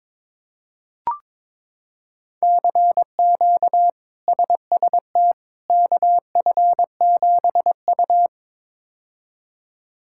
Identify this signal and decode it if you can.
Morse code, heard as a keyed tone, about 22 wpm, 700 Hz. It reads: CQ SST KF7U
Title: K1USN SST QSO Practice - 22wpm